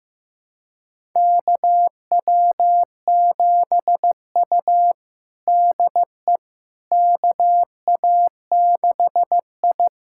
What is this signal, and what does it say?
Morse code, heard as a keyed tone, about 15 wpm, 700 Hz. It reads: KW7U DE KA6I
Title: USA Callsign DE Callsign - 4 Characters - 15wpm